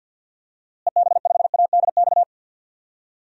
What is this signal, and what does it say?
Morse code, heard as a keyed tone, about 53 wpm, 700 Hz. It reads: EB5ADX
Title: DXCC-100 Call Signs - 6 Characters - Repeated 2x - 50wpm